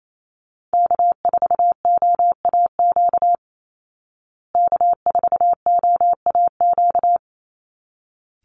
Morse code, heard as a keyed tone, about 28 wpm, 700 Hz. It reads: K4OAQ K4OAQ
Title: Call Signs Repeated 3x - 28wpm